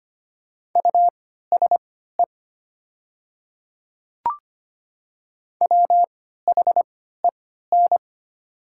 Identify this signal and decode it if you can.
Morse code, heard as a keyed tone, about 25 wpm, spaced as for 14 wpm, 700 Hz. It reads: USE WHEN
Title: 3x Character Spacing - Top 200 Words - 25wpm